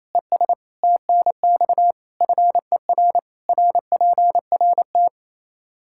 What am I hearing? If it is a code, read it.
Morse code, heard as a keyed tone, about 28 wpm, 700 Hz. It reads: ES TNX FER RPRT
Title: Practice QSO - Exchange 2 - Info - 28wpm